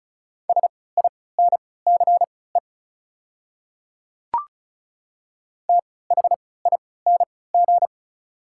Morse code, heard as a keyed tone, about 36 wpm, spaced as for 20 wpm, 700 Hz. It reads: SINCE THING
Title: 3x Character Spacing - Top 300 Words - 35wpm